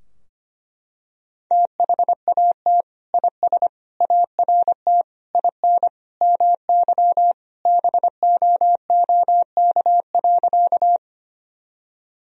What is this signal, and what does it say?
Morse code, heard as a keyed tone, about 25 wpm, 700 Hz. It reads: THAT IS ART IN MY BOOK.
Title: Sentences from Top 1000 Words - Review Part 3 - 25wpm